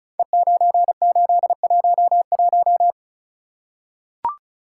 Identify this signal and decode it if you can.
Morse code, heard as a keyed tone, about 36 wpm, 700 Hz. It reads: E9811
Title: Code Groups - 5 Letter:Numbers - 35wpm